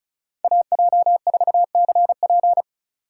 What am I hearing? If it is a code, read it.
Morse code, heard as a keyed tone, about 34 wpm, 700 Hz. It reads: AJ4CP
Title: USA Call Signs - 5 Characters - 2x Character Spacing - 35wpm